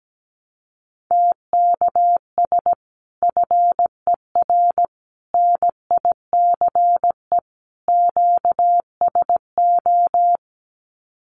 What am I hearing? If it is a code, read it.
Morse code, heard as a keyed tone, about 17 wpm, 700 Hz. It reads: TKS FER NICE QSO
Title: Common QSO Phrases 17wpm